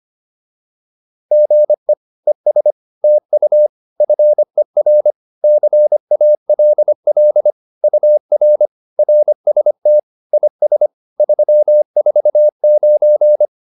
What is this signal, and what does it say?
Morse code, heard as a keyed tone, about 25 wpm, 600 Hz. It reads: GE ES TU FER CALL UR RST IS 349
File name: QSO106 25